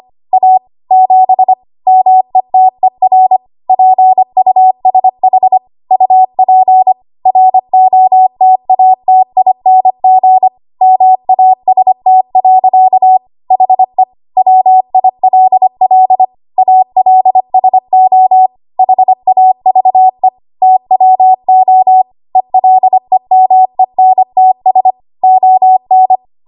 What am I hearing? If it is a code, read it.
Morse code, heard as a keyed tone, about 25 wpm, 750 Hz. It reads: A 7 METER PUSH UP ROTATING MAST. HE WILL ALSO HAVE TWO ELEMENTS ON
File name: Archive / 25 WPM / 25WPM